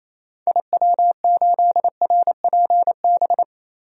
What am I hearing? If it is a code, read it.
Morse code, heard as a keyed tone, about 28 wpm, 700 Hz. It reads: IW8RPB
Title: DXCC-100 Call Signs - 6 Characters - 28wpm